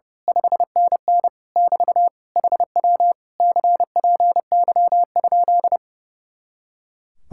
Morse code, heard as a keyed tone, about 28 wpm, 700 Hz. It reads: 5NN <BT> HW CPY?
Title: Speed-Racing Common QSO Phrases - 30wpm